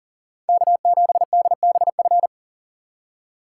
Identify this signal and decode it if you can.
Morse code, heard as a keyed tone, about 49 wpm, 700 Hz. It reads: K7DBF
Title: Speed Racing - USA 1x3 Call Signs 40wpm